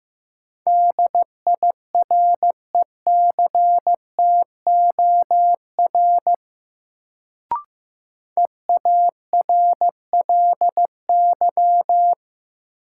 Morse code, heard as a keyed tone, about 15 wpm, 700 Hz. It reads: DIRECTOR EARLY
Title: Sets of 3 Words - New Word Drill - 15wpm